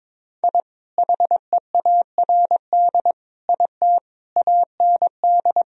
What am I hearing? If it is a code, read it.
Morse code, heard as a keyed tone, about 22 wpm, 700 Hz. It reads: I HEARD IT AND
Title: Sentences from Top 1000 Words - Review Part 1 - 22wpm